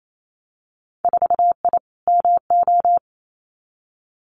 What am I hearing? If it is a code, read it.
Morse code, heard as a keyed tone, about 28 wpm, 700 Hz. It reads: 4I MO
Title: Winter Field Day Exchanges 28wpm